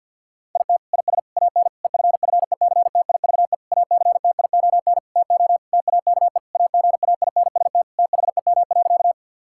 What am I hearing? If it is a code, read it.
Morse code, heard as a keyed tone, about 52 wpm, 700 Hz. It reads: IT IS AN EFFECTIVE ACTION TO TAKE AGAINST THEM.
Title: Sentences from Top 1000 Words - Review Part 6 - 50wpm